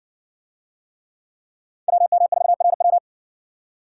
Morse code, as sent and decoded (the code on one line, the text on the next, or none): .-- -- ....- .-. .--
WM4RW